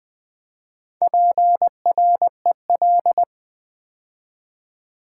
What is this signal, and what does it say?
Morse code, heard as a keyed tone, about 20 wpm, 700 Hz. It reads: PREL
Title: Build A Word - 4 Letters At A Time - 11-Letter Words - 20wpm